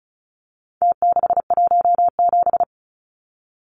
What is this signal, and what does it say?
Morse code, heard as a keyed tone, about 34 wpm, 700 Hz. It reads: T617
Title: Numbers - Contest Style - 4-Digit 35wpm